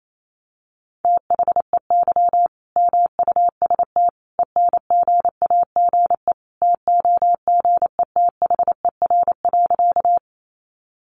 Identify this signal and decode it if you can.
Morse code, heard as a keyed tone, about 28 wpm, 700 Hz. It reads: THEY MUST ENGAGE TOGETHER.
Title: Sentences from Top 1500 Words 28wpm